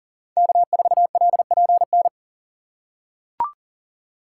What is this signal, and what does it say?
Morse code, heard as a keyed tone, about 40 wpm, 700 Hz. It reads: K4LPN
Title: USA Call Signs - 5 Characters - 4x Character Spacing - 40wpm